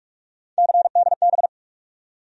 Morse code, heard as a keyed tone, about 46 wpm, 700 Hz. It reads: CDB
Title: USA Call Sign Postfixes - 3 Letters 45wpm